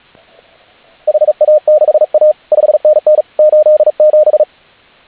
Morse code, heard as a keyed tone, about 39 wpm, 590 Hz, with radio noise. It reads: HA6A 5NN 87